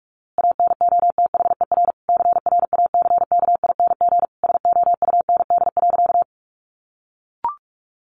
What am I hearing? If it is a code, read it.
Morse code, heard as a keyed tone, about 44 wpm, 700 Hz. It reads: ANOTHER CRACKING SOUND.
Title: A Sound Of Thunder 45wpm